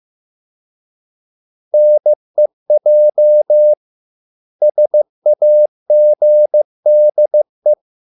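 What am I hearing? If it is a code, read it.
Morse code, heard as a keyed tone, about 15 wpm, 600 Hz. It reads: NEJ SAGDE
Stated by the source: Morse code at a slow 15 wpm from CW med Gnister.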